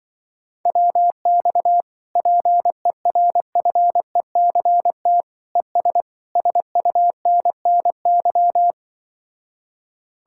Morse code, heard as a keyed tone, about 24 wpm, 700 Hz. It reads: WX PERFECT ES SUNNY
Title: Speed-Racing Common QSO Phrases - 20wpm